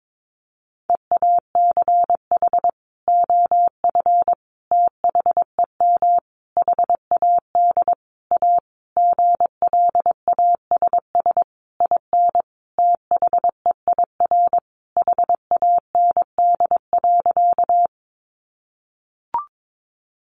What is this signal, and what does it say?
Morse code, heard as a keyed tone, about 22 wpm, 700 Hz. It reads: EACH OF THEM HAD A GLASS IN THEIR HAND.
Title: Sentences from Top 1300 Words 22wpm